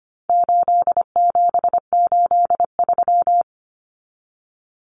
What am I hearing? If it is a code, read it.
Morse code, heard as a keyed tone, about 25 wpm, 700 Hz. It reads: 8783
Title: Common Cut Numbers - 4-Digit 25wpm